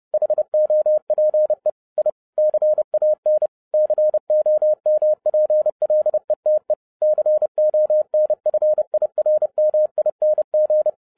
Morse code, heard as a keyed tone, about 30 wpm, 610 Hz. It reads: HOPE I CAN COMPLETE CONFIRMING
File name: 2012-01-mp3 / n12-I DIDNT MISS THE SOLAR CYCLE 24-30wpm